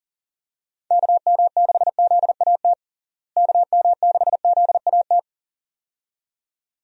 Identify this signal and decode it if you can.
Morse code, heard as a keyed tone, about 40 wpm, 700 Hz. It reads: KM6ZAT KM6ZAT
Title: USA Call Signs - 6 Characters - Repeated 2x - 40wpm